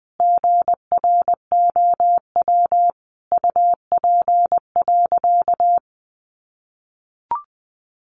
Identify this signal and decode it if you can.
Morse code, heard as a keyed tone, about 20 wpm, 700 Hz. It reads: GROW UP.
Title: Sentences from Top 300 Words 20wpm